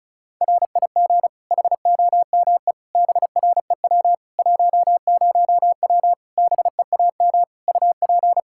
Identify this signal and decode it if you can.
Morse code, heard as a keyed tone, about 34 wpm, 700 Hz. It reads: RIG HOME BREW 10W BEAM UP
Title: Practice QSO - Exchange 2 - Info - 35wpm